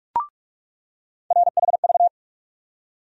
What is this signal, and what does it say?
Morse code, heard as a keyed tone, about 47 wpm, 700 Hz. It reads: RHV
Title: USA Call Sign Postfixes - 3 Letters 45wpm